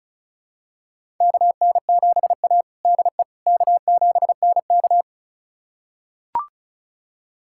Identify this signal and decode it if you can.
Morse code, heard as a keyed tone, about 36 wpm, 700 Hz. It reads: KN7A DE K7NK
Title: USA Callsign DE Callsign - 4 Characters - 35wpm